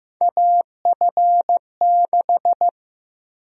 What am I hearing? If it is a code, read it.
Morse code, heard as a keyed tone, about 15 wpm, 700 Hz. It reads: AF6
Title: USA Call Sign Prefixes - Repeated 2x - 15wpm